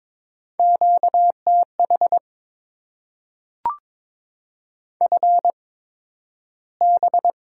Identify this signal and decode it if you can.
Morse code, heard as a keyed tone, about 22 wpm, spaced as for 12 wpm, 700 Hz. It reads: QTH FB
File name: Top 20 QSO Elements - 8x Character Spacing - 22wpm